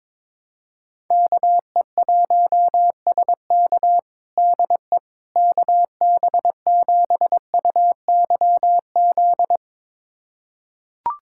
Morse code, heard as a keyed tone, about 22 wpm, 700 Hz. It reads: KE1SK DE KB7UYZ
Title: USA Callsign DE Callsign 22wpm